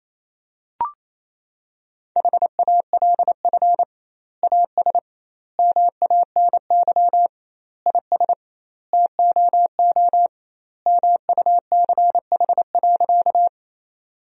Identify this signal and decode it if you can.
Morse code, heard as a keyed tone, about 28 wpm, 700 Hz. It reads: HALF AS MANY IS TOO MUCH.
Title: 2x Word-Spacing - Sentences from Top 400 Words - 28wpm